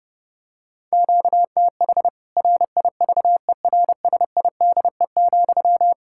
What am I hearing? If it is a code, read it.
Morse code, heard as a keyed tone, about 30 wpm, 700 Hz. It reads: QTH RIVERSIDE,
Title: Practice QSO - Exchange 1 - Beginning - 30wpm